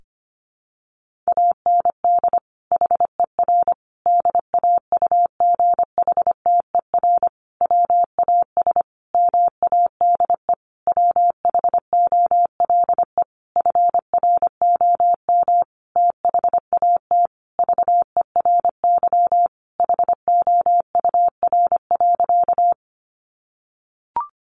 Morse code, heard as a keyed tone, about 25 wpm, 700 Hz. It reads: AND HER DAUGHTER WAS MADE WHOLE FROM THAT VERY HOUR.
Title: Sentences from Top 1500 Words 25wpm